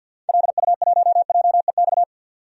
Morse code, heard as a keyed tone, about 46 wpm, 700 Hz. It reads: LU1JEX
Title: DXCC-100 Call Signs - 6 Characters - 50wpm